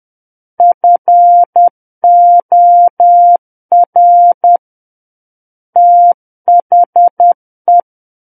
Morse code, heard as a keyed tone, about 10 wpm, 700 Hz. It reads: FOR THE
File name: Book One / 01-Chapter 1 - 10 WPM